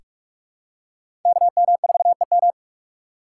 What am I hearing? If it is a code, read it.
Morse code, heard as a keyed tone, about 43 wpm, 700 Hz. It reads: KM4EM
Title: USA Call Signs - 6x Character Spacing - 45wpm